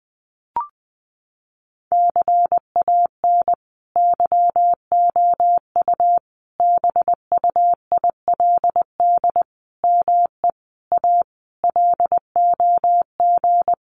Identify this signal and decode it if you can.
Morse code, heard as a keyed tone, about 20 wpm, 700 Hz. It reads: CAN YOU BUILD ME A LOG
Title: Sentences from Top 1400 Words 20wpm